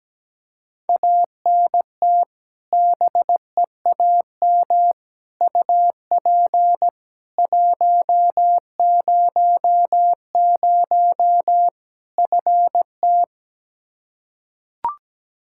Morse code, heard as a keyed tone, about 17 wpm, 700 Hz. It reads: ANT BEAM UP 100 FT
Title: K1USN SST QSO Practice 17wpm